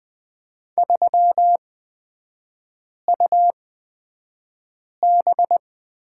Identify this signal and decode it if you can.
Morse code, heard as a keyed tone, about 20 wpm, spaced as for 5 wpm, 690 Hz. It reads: 3UB
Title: 255 Lesson 41e - Period - Three Characters 20wpm